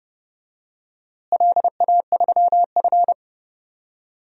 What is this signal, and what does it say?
Morse code, heard as a keyed tone, about 30 wpm, 700 Hz. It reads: LA3F
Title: DXCC-100 Call Signs - 4 Characters - Repeated 2x - 30wpm